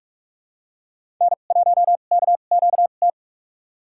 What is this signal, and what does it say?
Morse code, heard as a keyed tone, about 47 wpm, spaced as for 33 wpm, 700 Hz. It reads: N1KQT
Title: USA Call Signs - 5 Characters - 2x Character Spacing - 45wpm